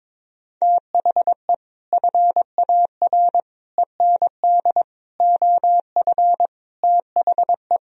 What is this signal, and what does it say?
Morse code, heard as a keyed tone, about 22 wpm, 700 Hz. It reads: THE FAR END OF THE
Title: Speed-Racing - Sets of 5 Words 22wpm